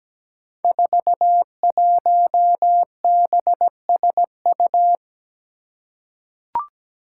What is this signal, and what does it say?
Morse code, heard as a keyed tone, about 17 wpm, 700 Hz. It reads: 41BSU
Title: Five Challenging Letter-Number Combinations 17wpm